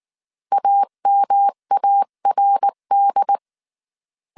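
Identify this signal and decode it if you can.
Morse code, heard as a keyed tone, about 19 wpm, 800 Hz. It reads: AMARD
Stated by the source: Morse code dictation at 18 wpm from the IW3SGT QRS Morse code training course, five-character groups of 300 characters.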